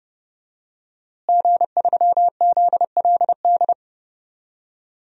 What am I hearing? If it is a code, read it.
Morse code, heard as a keyed tone, about 32 wpm, 700 Hz. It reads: G3ZLD
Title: DXCC-100 Call Signs - 5 Characters - Repeated 3x - 30wpm